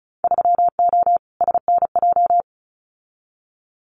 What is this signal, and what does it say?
Morse code, heard as a keyed tone, about 34 wpm, 700 Hz. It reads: 3O SNJ
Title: Winter Field Day Exchanges Encore 35wpm